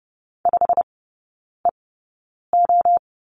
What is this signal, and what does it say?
Morse code, heard as a keyed tone, about 30 wpm, spaced as for 9 wpm, 700 Hz. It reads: 5EO
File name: Lesson 17e - Letter u - Three Characters 30wpm